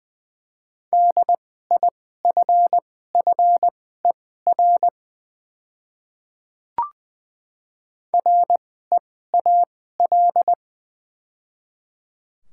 Morse code, heard as a keyed tone, about 20 wpm, spaced as for 14 wpm, 700 Hz. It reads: DIFFER REAL
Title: 2x Character Spacing - Top 300 Words - 20wpm